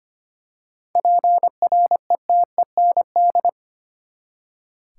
Morse code, heard as a keyed tone, about 25 wpm, 700 Hz. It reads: PRETEND
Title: Seven Letter Words - 25wpm